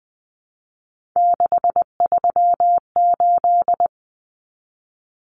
Morse code, heard as a keyed tone, about 20 wpm, 700 Hz. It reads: 638
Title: Numbers - 3-Digit 20wpm